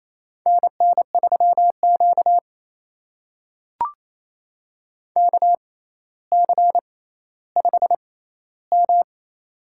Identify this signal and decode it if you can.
Morse code, heard as a keyed tone, about 28 wpm, spaced as for 14 wpm, 700 Hz. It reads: NN3Q KC5M
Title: USA Call Signs - 4 Characters - 6x Character Spacing - 28wpm